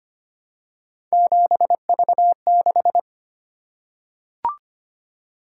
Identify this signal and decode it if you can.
Morse code, heard as a keyed tone, about 25 wpm, 700 Hz. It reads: 7V6
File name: Three Letter-Number Combinations - 25wpm